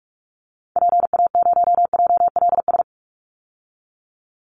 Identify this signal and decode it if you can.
Morse code, heard as a keyed tone, about 44 wpm, 700 Hz. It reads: PA0JLS
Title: Call Signs Repeated 3x - 45wpm